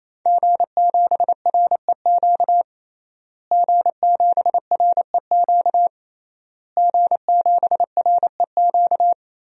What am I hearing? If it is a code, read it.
Morse code, heard as a keyed tone, about 28 wpm, 700 Hz. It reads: G7REQ G7REQ G7REQ
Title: DXCC-100 Call Signs - 5 Characters - Repeated 3x - 28wpm